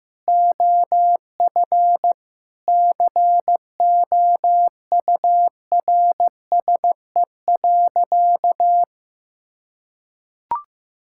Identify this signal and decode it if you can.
Morse code, heard as a keyed tone, about 15 wpm, 700 Hz. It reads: OF COURSE.
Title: Sentences from Top 1000 Words - Review Part 5 - 15wpm